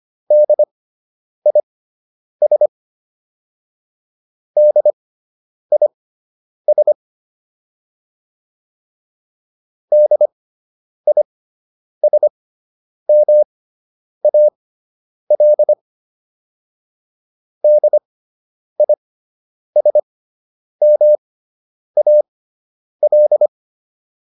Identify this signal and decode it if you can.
Morse code, heard as a keyed tone, about 25 wpm, spaced as for 9 wpm, 600 Hz. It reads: DIS DIS DISMAL DISMAL